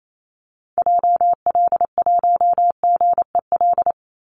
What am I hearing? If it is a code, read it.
Morse code, heard as a keyed tone, about 28 wpm, 700 Hz. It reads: JL1GEL
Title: Call Signs - 28wpm